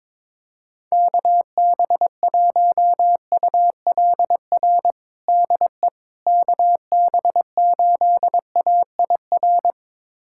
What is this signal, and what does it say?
Morse code, heard as a keyed tone, about 22 wpm, 700 Hz. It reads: KB1ULR DE KB8AIR
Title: USA Callsign DE Callsign 22wpm